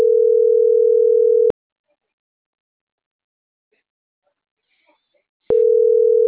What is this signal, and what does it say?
Ringback tone (North America), 440 + 480 Hz.